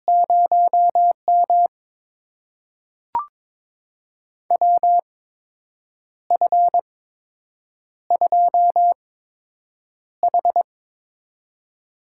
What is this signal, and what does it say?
Morse code, heard as a keyed tone, about 22 wpm, spaced as for 7 wpm, 700 Hz. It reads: 0M WF2H
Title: USA Call Signs - 4 Characters - 8x Character Spacing - 22wpm